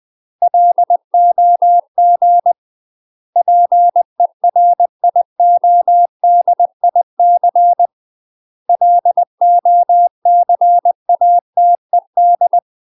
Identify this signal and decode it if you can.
Morse code, heard as a keyed tone, about 20 wpm, 700 Hz. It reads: LOG PERIODIC LOCATED
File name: UK MASTER QSOs TRAINING / 20WPM / UK MASTER QSOs 0032